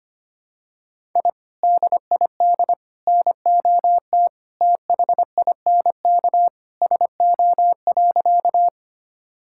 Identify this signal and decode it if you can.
Morse code, heard as a keyed tone, about 25 wpm, 700 Hz. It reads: I DID NOT THINK SO.
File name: practice / 25wpm / Sentences with 2-Word Phrases - 25wpm